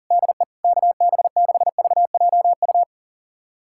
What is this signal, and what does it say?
Morse code, heard as a keyed tone, about 40 wpm, 700 Hz. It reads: DE KB6VJU
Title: USA Callsign DE Callsign - 6 Characters - 40wpm